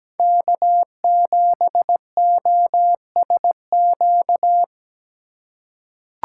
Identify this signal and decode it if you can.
Morse code, heard as a keyed tone, about 17 wpm, 700 Hz. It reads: K7OSQ
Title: USA Call Signs - 5 Characters - 2x Character Spacing - 17wpm